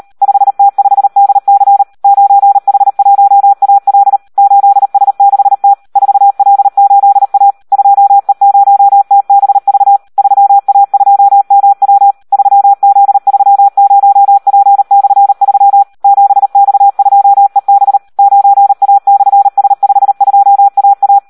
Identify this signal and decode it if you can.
Morse code, heard as a keyed tone, about 38 wpm, 800 Hz. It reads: 5T5DC 9H1AL 8S6T 4L8A 2E0TBV 3A2MW 3Z30P/3 7X2EB 9A/S52AA
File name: Audio practica CW / SD 1 38